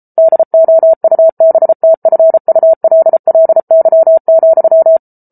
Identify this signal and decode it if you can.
Morse code, heard as a keyed tone, about 34 wpm, 650 Hz. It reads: DOUBTFULLY,